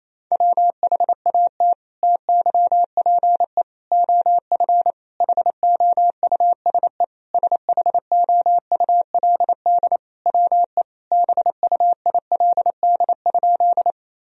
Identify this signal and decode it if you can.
Morse code, heard as a keyed tone, about 28 wpm, 700 Hz. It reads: WHAT TYPE OF HOUSE SHOULD WE BUILD?
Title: Sentences from Top 1000 Words - Review Part 3 - 28wpm